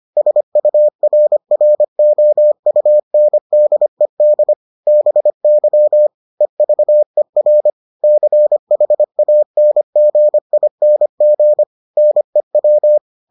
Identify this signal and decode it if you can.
Morse code, heard as a keyed tone, about 25 wpm, 600 Hz. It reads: SURROUNDED BY EVER CHANGING NEW